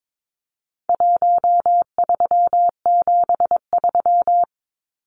Morse code, heard as a keyed tone, about 22 wpm, 700 Hz. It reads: 1373
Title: Numbers - Contest Style - 4-Digit 22wpm